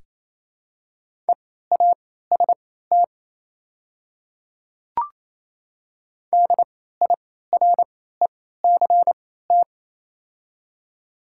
Morse code, heard as a keyed tone, about 28 wpm, spaced as for 16 wpm, 700 Hz. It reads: EAST DIRECT
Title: 3x Character Spacing - Top 500 Words - 28wpm